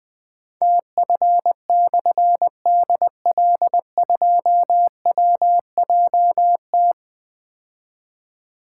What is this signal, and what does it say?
Morse code, heard as a keyed tone, about 20 wpm, 700 Hz. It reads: TF/DL2WJT
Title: Difficult Call Signs 20wpm